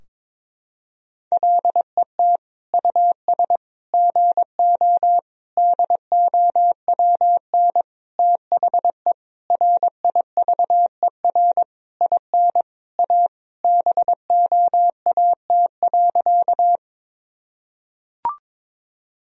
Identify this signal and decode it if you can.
Morse code, heard as a keyed tone, about 22 wpm, 700 Hz. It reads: LET US GO DOWN THE RIVER IN A BOAT.
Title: Sentences from Top 1000 Words - Review Part 3 - 22wpm